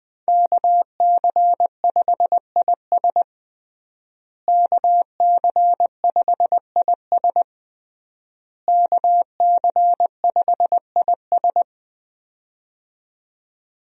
Morse code, heard as a keyed tone, about 21 wpm, 700 Hz. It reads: KC5IS KC5IS KC5IS
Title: USA Call Signs - 5 Characters - Repeated 3x - 20wpm